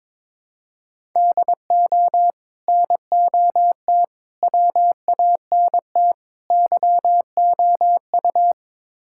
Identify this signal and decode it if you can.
Morse code, heard as a keyed tone, about 22 wpm, 700 Hz. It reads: DO NOT WANT YOU
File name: practice / 22wpm / Build A Phrase - 5-Word Phrases - 22wpm